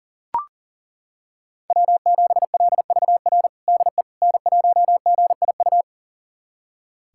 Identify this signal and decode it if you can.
Morse code, heard as a keyed tone, about 37 wpm, 690 Hz. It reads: W7LVR DE N1GIU
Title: USA Callsign DE Callsign - 5 Characters - 40wpm